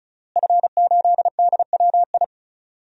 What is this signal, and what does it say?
Morse code, heard as a keyed tone, about 36 wpm, 700 Hz. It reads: F8DWI
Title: DXCC-100 Call Signs - 5 Characters - Repeated 3x - 35wpm